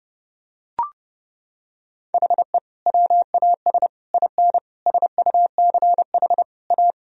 Morse code, heard as a keyed tone, about 30 wpm, 700 Hz. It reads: HE WAS IN SUCH A
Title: Sentences from Top 1000 Words - Review Part 2 - 30wpm